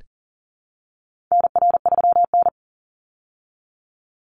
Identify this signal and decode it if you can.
Morse code, heard as a keyed tone, about 40 wpm, 700 Hz. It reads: NR3N
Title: Call Signs Repeated 3x - 40wpm